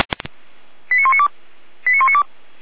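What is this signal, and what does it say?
Five-tone selective call (ZVEI), heard as quick runs of five tones, tones of about 80 ms. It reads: 89192 89192
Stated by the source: ZVEI five-tone selective call received on SDR.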